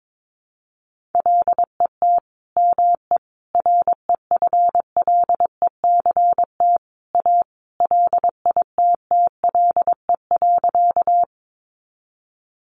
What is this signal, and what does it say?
Morse code, heard as a keyed tone, about 22 wpm, 700 Hz. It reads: LET ME REFLECT A LITTLE.
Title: Sentences from Top 1500 Words 22wpm